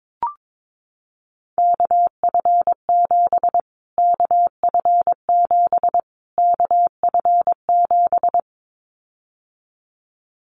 Morse code, heard as a keyed tone, about 22 wpm, 700 Hz. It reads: KF7 KF7 KF7
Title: Callsign Prefixes for Top 25 DXCC - Repeated 3x - 22wpm